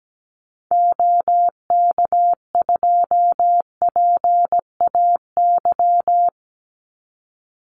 Morse code, heard as a keyed tone, about 17 wpm, 700 Hz. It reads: OK2PAY
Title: Call Signs Repeated 3x - 17wpm